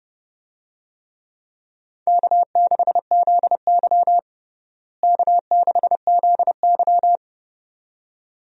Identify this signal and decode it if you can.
Morse code, heard as a keyed tone, about 31 wpm, 700 Hz. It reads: K6ZY K6ZY
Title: USA Call Signs - 4 Characters - Repeated 2x - 30wpm